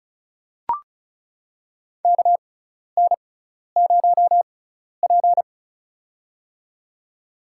Morse code, heard as a keyed tone, about 36 wpm, spaced as for 12 wpm, 700 Hz. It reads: KN0P
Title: USA Call Signs - 4 Characters - 6x Character Spacing - 35wpm